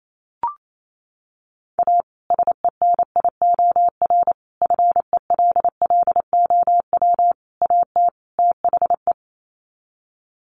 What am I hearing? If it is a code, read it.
Morse code, heard as a keyed tone, about 28 wpm, 700 Hz. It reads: A SENIOR FELLOW AT THE
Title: Sets of 5 Words Encore 28wpm